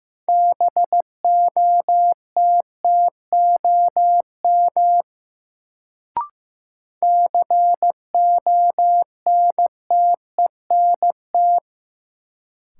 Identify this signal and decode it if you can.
Morse code, heard as a keyed tone, about 15 wpm, 700 Hz. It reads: BOTTOM CONTENT
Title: Sentences from Top 900 Words - New Word Drill - 15wpm